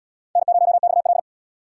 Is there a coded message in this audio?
Yes: Morse code from 0.4 to 1.2 s, heard as a keyed tone, about 77 wpm, 700 Hz. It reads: N0XL